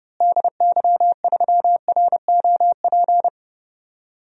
Morse code, heard as a keyed tone, about 30 wpm, 700 Hz. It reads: DY3ROP
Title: DXCC-100 Call Signs - 6 Characters - Repeated 2x - 30wpm